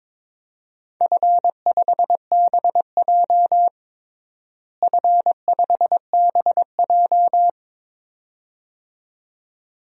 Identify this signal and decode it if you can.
Morse code, heard as a keyed tone, about 23 wpm, 700 Hz. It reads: F5BJ F5BJ
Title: DXCC-100 Call Signs - 4 Characters - Repeated 2x - 22wpm